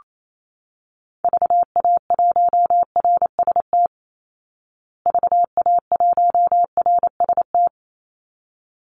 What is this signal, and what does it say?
Morse code, heard as a keyed tone, about 28 wpm, 700 Hz. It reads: VA1RST VA1RST
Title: Call Signs Repeated 3x - 28wpm